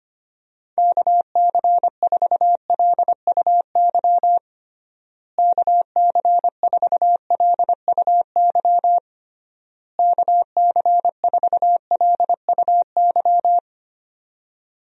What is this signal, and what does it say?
Morse code, heard as a keyed tone, about 25 wpm, 700 Hz. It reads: KC4LUY KC4LUY KC4LUY
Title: USA Call Signs - 6 Characters - Repeated 3x - 25wpm